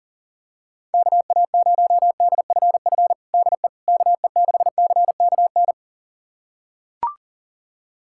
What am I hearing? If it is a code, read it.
Morse code, heard as a keyed tone, about 40 wpm, 700 Hz. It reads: KA0DFF DE KE6CKN
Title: USA Callsign DE Callsign 40wpm